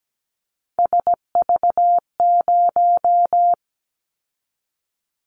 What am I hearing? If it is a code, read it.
Morse code, heard as a keyed tone, about 17 wpm, 700 Hz. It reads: SV0
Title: Callsign Prefixes for Top 50 DXCC - 17wpm